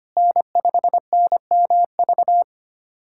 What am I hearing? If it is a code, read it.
Morse code, heard as a keyed tone, about 25 wpm, 700 Hz. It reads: N5NMV
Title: USA Callsign DE Callsign - 5 Characters - 25wpm